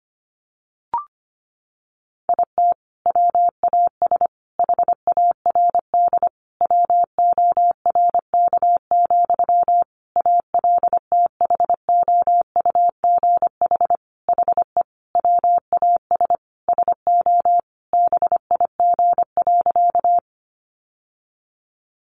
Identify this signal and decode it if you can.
Morse code, heard as a keyed tone, about 25 wpm, 700 Hz. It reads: IT WAS HARD WORK, ALTHOUGH HE WAS SO BIG.
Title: Sentences from Top 600 Words 25wpm